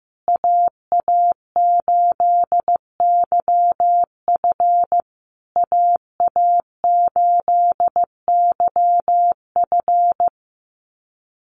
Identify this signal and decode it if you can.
Morse code, heard as a keyed tone, about 15 wpm, 700 Hz. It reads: AA8YF AA8YF
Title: USA Call Signs Repeated 3x 15wpm